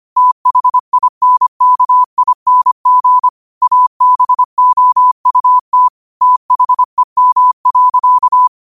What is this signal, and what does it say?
Morse code, heard as a keyed tone, about 25 wpm, 1000 Hz. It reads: THINKING ABOUT THEM.
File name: 25wpm